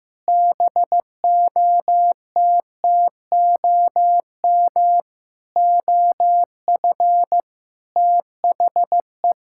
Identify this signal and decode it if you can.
Morse code, heard as a keyed tone, about 15 wpm, 700 Hz. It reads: BOTTOM OF THE
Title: Sentences from Top 1000 Words - Review Part 6 - 15wpm